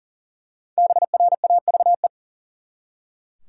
Morse code, heard as a keyed tone, about 39 wpm, 700 Hz. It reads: BRAVE
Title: Five Letter Words - 40wpm